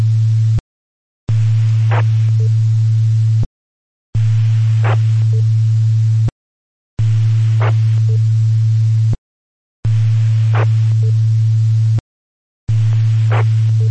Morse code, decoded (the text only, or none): none